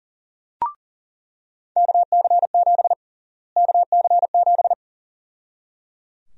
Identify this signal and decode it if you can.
Morse code, heard as a keyed tone, about 37 wpm, 700 Hz. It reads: KC7 KC7
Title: USA Call Sign Prefixes - Repeated 2x - 40wpm